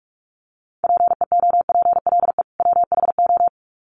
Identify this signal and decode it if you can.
Morse code, heard as a keyed tone, about 44 wpm, 700 Hz. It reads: PEOPLE WHO
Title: Sets of 2 Words 45wpm